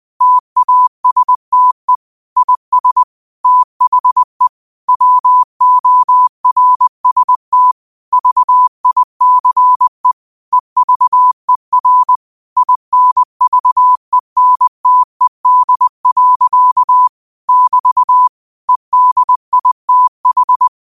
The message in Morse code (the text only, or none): TASTE IS THE WORST VICE EVER INVENTED. <BT> EDITH